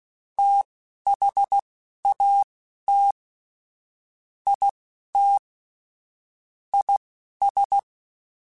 Morse code, heard as a keyed tone, about 16 wpm, spaced as for 12 wpm, 800 Hz. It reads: THAT IT IS
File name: Technical Subjects / Technical Subjects-mp3s / OR Gate @10